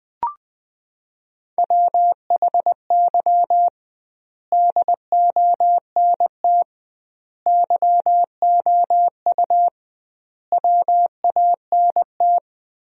Morse code, heard as a keyed tone, about 20 wpm, 700 Hz. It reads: WHY DONT YOU WANT
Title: 2x Word-Spacing - Sentences from Top 300 Words - 20wpm